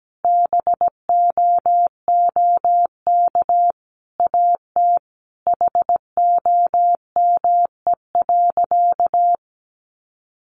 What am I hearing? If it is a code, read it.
Morse code, heard as a keyed tone, about 17 wpm, 700 Hz. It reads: BOOK AT HOME.
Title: Sentences from Top 1400 Words 17wpm